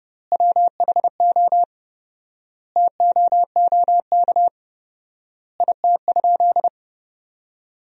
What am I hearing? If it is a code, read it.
Morse code, heard as a keyed tone, about 32 wpm, 700 Hz. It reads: WHO TOOK IT?
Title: 4x Word-Spacing - Sentences from Top 300 Words - 30wpm